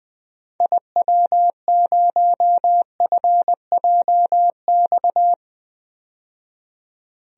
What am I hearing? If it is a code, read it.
Morse code, heard as a keyed tone, about 20 wpm, 700 Hz. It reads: IW0FJX
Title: DXCC-100 Call Signs - 6 Characters - 20wpm